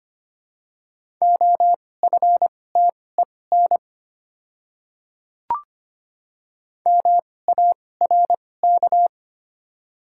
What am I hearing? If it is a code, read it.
Morse code, heard as a keyed tone, about 25 wpm, spaced as for 18 wpm, 700 Hz. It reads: OFTEN MARK
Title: 2x Character Spacing - Top 300 Words - 25wpm